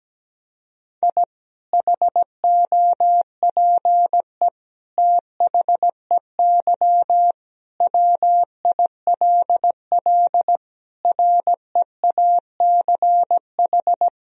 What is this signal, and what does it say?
Morse code, heard as a keyed tone, about 17 wpm, 700 Hz. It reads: I HOPE THEY WILL REACH
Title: Sentences from Top 1000 Words - Review Part 2 - 17wpm